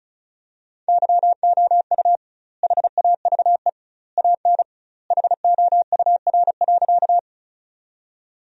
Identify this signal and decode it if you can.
Morse code, heard as a keyed tone, about 34 wpm, 700 Hz. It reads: YOU HAVE AN HOUR.
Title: 2x Word-Spacing - Sentences from Top 400 Words - 35wpm